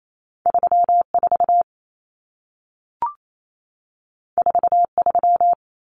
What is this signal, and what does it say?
Morse code, heard as a keyed tone, about 28 wpm, 700 Hz. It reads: 34 43
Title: Common Cut Numbers - 2-Digit 28wpm